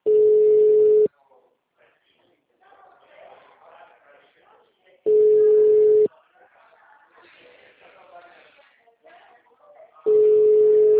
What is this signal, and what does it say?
Ringback tone (Europe, 425 Hz), with speech.